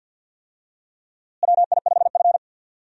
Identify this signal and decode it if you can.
Morse code, heard as a keyed tone, about 53 wpm, 700 Hz. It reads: WI5F
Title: USA Call Signs - 4 Characters - Repeated 3x - 50wpm